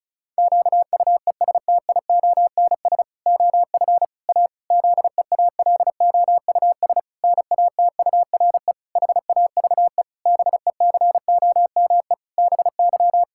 Morse code, heard as a keyed tone, about 36 wpm, 700 Hz. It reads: QUESTIONS OF A ZEALOUS NATURE HAVE BECOME BY
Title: Pangram Challenge v2 - 35wpm